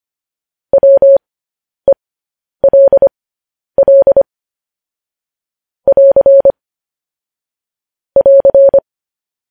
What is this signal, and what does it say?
Morse code, heard as a keyed tone, about 25 wpm, spaced as for 10 wpm, 550 Hz. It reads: WELL <AR> <AR>
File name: podcast / News headlines-1x-25x10